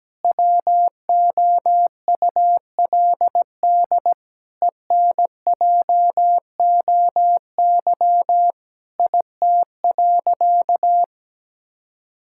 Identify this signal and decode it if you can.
Morse code, heard as a keyed tone, about 17 wpm, 700 Hz. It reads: WOULD ENJOY IT.
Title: Sentences from Top 1000 Words - Review Part 5 - 17wpm